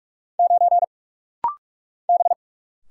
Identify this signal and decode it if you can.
Morse code, heard as a keyed tone, about 46 wpm, 700 Hz. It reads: 9 B
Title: Single Letter-Number Alt - Rapid-Fire 45wpm